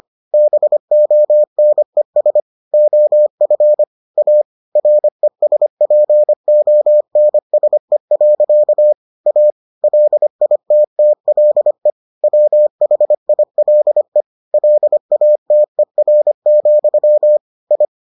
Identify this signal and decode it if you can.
Morse code, heard as a keyed tone, about 25 wpm, 600 Hz. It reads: BONES OF A RESPONSE. A LITTLE WHILE LATER, I